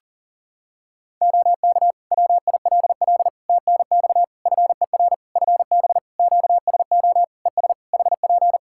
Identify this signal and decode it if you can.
Morse code, heard as a keyed tone, about 40 wpm, 700 Hz. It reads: OK WILL TNX FER FB QSO ES HP
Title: Practice QSO - Exchange 4 - Ending - 40wpm